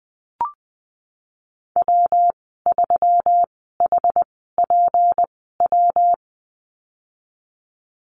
Morse code, heard as a keyed tone, about 20 wpm, spaced as for 15 wpm, 700 Hz. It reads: W3HPW
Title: 179 Lesson 29g - Slash - Call Signs 20wpm